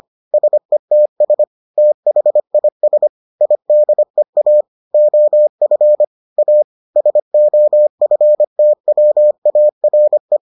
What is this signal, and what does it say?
Morse code, heard as a keyed tone, about 25 wpm, 600 Hz. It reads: SETS THIS IDEA OF A SOFTWARE